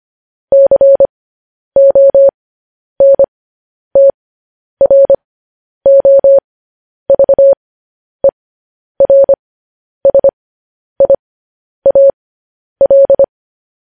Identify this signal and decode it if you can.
Morse code, heard as a keyed tone, about 25 wpm, spaced as for 10 wpm, 550 Hz. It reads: CONTROVERSIAL
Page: news headlines in morse code at 10 wpm